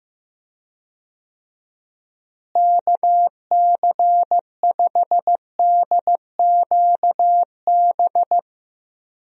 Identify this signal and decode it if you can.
Morse code, heard as a keyed tone, about 15 wpm, 700 Hz. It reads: KC5DQB
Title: USA Call Signs - 6 Characters - Repeated 3x - 15wpm